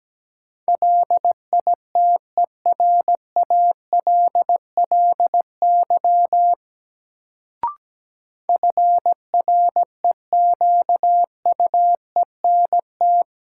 Morse code, heard as a keyed tone, about 17 wpm, 700 Hz. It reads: LITERALLY FREQUENT
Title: Sentences from Top 1400 Words - New Word Drill - 17wpm